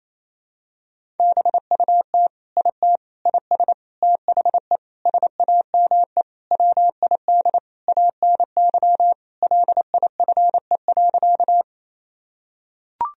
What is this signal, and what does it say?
Morse code, heard as a keyed tone, about 28 wpm, 700 Hz. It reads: BUT IT IS THE SAME WID ANY LIFE.
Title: Sentences from Top 300 Words and Abbreviations 28wpm